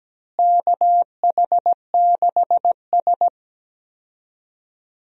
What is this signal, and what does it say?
Morse code, heard as a keyed tone, about 17 wpm, 700 Hz. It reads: KH6S
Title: DXCC-100 Call Signs - 4 Characters - Repeated 3x - 17wpm